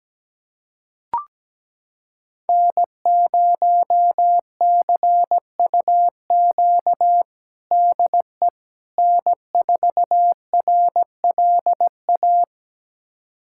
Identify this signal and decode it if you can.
Morse code, heard as a keyed tone, about 17 wpm, 700 Hz. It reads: N0CUQ DE N4RLA
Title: USA Callsign DE Callsign - 5 Characters - 17wpm